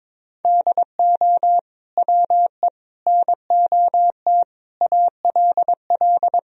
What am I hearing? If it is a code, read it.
Morse code, heard as a keyed tone, about 22 wpm, 700 Hz. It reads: DO WE NOT ALL
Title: Sentences from Top 1000 Words - Review Part 3 - 22wpm